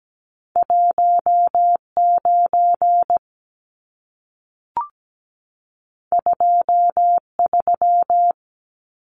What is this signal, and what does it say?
Morse code, heard as a keyed tone, about 17 wpm, 700 Hz. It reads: 19 23
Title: Numbers - 2-Digit 17wpm